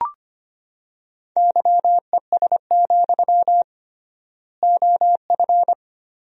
Morse code, heard as a keyed tone, about 25 wpm, 700 Hz. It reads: YES, OF
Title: 3x Word-Spacing - Sentences from Top 500 Words - 25wpm